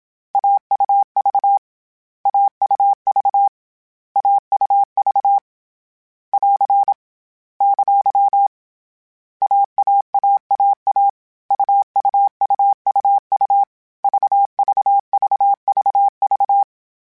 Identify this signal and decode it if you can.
Morse code, heard as a keyed tone, about 27 wpm, 800 Hz. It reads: AUV AUV AUV <AR> ! AAAAA UUUUU VVVVV
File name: CW Code Groups-mp3s / Sets mp3s / AUV AUV4 sp @25